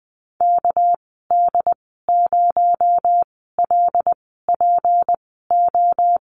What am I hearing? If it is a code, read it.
Morse code, heard as a keyed tone, about 20 wpm, spaced as for 14 wpm, 700 Hz. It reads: KD0LPO
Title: 238 Lesson 38g - Letter q - Call Signs 20wpm